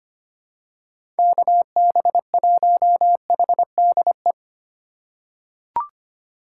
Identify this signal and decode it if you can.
Morse code, heard as a keyed tone, about 25 wpm, 700 Hz. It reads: KB1HDE
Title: USA Call Signs - 6 Characters - 3x Character Spacing - 25wpm